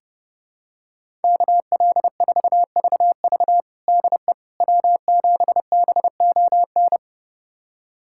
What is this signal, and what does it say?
Morse code, heard as a keyed tone, about 30 wpm, 700 Hz. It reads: KL4VV DE W7BON
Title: USA Callsign DE Callsign - 5 Characters - 30wpm